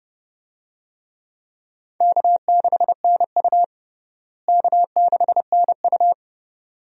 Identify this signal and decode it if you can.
Morse code, heard as a keyed tone, about 29 wpm, 700 Hz. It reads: K6NU K6NU
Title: USA Call Signs - 4 Characters - Repeated 2x - 30wpm